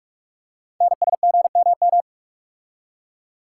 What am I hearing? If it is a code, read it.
Morse code, heard as a keyed tone, about 44 wpm, 700 Hz. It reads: NSGMM
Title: Code Groups - 5 Letters - 45wpm